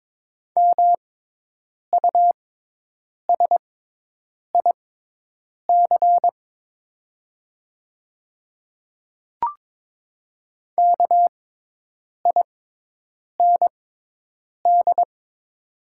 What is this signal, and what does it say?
Morse code, heard as a keyed tone, about 22 wpm, spaced as for 8 wpm, 700 Hz. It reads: MUSIC KIND
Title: 6x Character Spacing - Top 500 Words - 22wpm